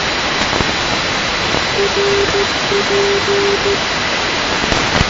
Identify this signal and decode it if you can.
Morse code, heard as a keyed tone, about 13 wpm, 400 Hz, with noise. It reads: RP
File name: asia / tkm / tkm-29-ndb dashoguz rp-322-2019-rb-uzb